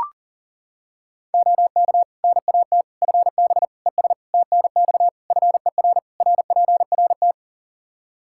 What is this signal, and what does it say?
Morse code, heard as a keyed tone, about 40 wpm, 700 Hz. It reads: OK NAT FB ES TNX FER RPRT
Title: Practice QSO - Exchange 2 - Info - 40wpm